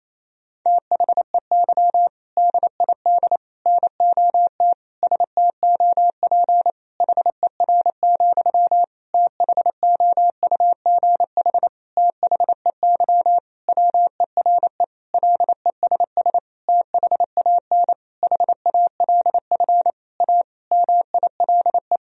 Morse code, heard as a keyed tone, about 28 wpm, 700 Hz. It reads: THEY DID NOT STOP HER, THOUGH THEY WERE LESS THAN HALF A MILE
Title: Sentences from Top 1000 Words - Review Part 2 - 28wpm